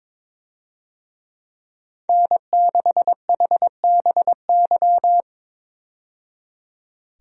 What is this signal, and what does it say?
Morse code, heard as a keyed tone, about 22 wpm, 700 Hz. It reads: N6HBY
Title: USA Call Signs - 5 Characters - 22wpm